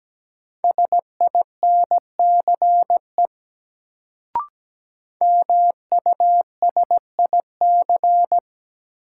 Morse code, heard as a keyed tone, about 17 wpm, 700 Hz. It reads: SINCE MUSIC
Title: Sentences from Top 300 Words - New Word Drill - 17wpm